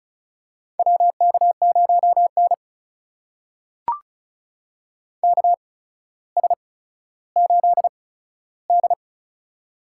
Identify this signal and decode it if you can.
Morse code, heard as a keyed tone, about 35 wpm, spaced as for 15 wpm, 700 Hz. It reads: WK0N KS8D
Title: USA Call Signs - 4 Characters - 8x Character Spacing - 35wpm